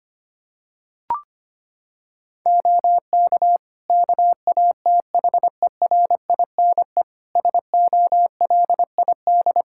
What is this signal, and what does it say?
Morse code, heard as a keyed tone, about 25 wpm, 700 Hz. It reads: OK KATHERINE SOLID
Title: K1USN SST QSO Practice 25wpm